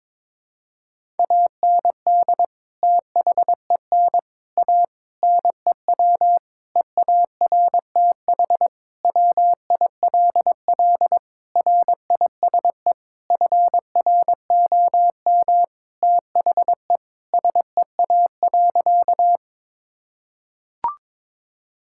Morse code, heard as a keyed tone, about 22 wpm, 700 Hz. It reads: AND THEN A NEW EARTH WILL RISE FROM THE SEA.
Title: Sentences from Top 1000 Words - Review Part 6 - 22wpm